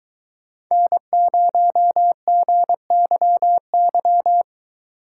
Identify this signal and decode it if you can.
Morse code, heard as a keyed tone, about 23 wpm, 700 Hz. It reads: N0GYY